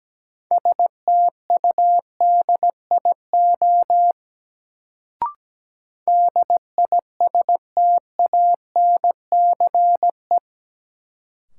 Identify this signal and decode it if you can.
Morse code, heard as a keyed tone, about 17 wpm, 700 Hz. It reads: STUDIO DISTANCE
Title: Sentences from Top 1200 Words - New Word Drill - 17wpm